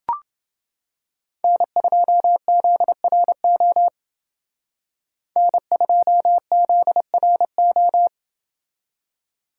Morse code, heard as a keyed tone, about 30 wpm, 700 Hz. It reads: N2ZRO N2ZRO